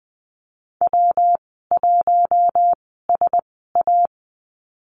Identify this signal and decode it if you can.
Morse code, heard as a keyed tone, about 19 wpm, spaced as for 14 wpm, 700 Hz. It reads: W1SA
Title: 122 Lesson 20g - Number 3 - Call Signs 20wpm